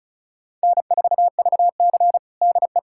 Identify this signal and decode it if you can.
Morse code, heard as a keyed tone, about 36 wpm, 700 Hz. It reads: N4VC DE
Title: USA Callsign DE Callsign - 4 Characters - 35wpm